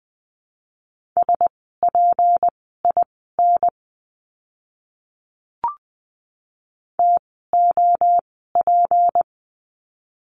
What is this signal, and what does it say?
Morse code, heard as a keyed tone, about 20 wpm, spaced as for 14 wpm, 700 Hz. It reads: SPIN TOP
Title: 153 Lesson 25f - Letter p - Words 20wpm